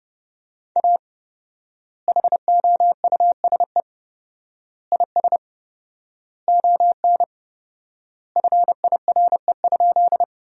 Morse code, heard as a keyed tone, about 30 wpm, 700 Hz. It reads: A HOUSE IS ON FIRE?
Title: 4x Word-Spacing - Sentences from Top 400 Words - 30wpm